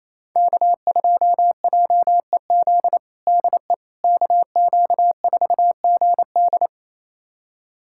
Morse code, heard as a keyed tone, about 27 wpm, 700 Hz. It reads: K2JEZ DE KQ4GD
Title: USA Callsign DE Callsign - 5 Characters - 28wpm